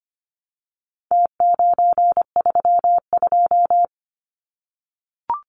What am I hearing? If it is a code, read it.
Morse code, heard as a keyed tone, about 25 wpm, 700 Hz. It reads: T932
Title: Numbers - Contest Style - 4-Digit 25wpm